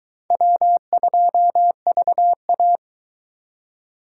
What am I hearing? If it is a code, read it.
Morse code, heard as a keyed tone, about 23 wpm, 700 Hz. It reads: W2VA